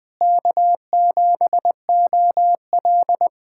Morse code, heard as a keyed tone, about 20 wpm, 700 Hz. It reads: K7OL
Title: USA Callsign DE Callsign - 4 Characters - 20wpm